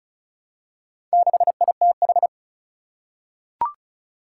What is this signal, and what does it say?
Morse code, heard as a keyed tone, about 36 wpm, 700 Hz. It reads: 6ITH
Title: Four Challenging Letter-Number Combinations 35wpm